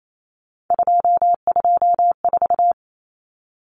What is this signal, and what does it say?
Morse code, heard as a keyed tone, about 28 wpm, 700 Hz. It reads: 224
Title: Common Cut Numbers - 3-Digit 28wpm